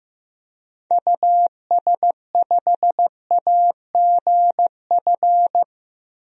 Morse code, heard as a keyed tone, about 15 wpm, 700 Hz. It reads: US5AGF
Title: DXCC-100 Call Signs - 6 Characters - Repeated 2x - 15wpm